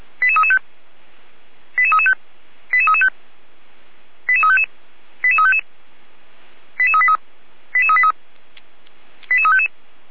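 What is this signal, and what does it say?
Five-tone selective call (ZVEI), heard as quick runs of five tones, tones of about 70 ms. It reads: 88335 88335 88335 88355 88355 88383 88383 88355
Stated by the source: ZVEI five-tone call 88327 received by SDR.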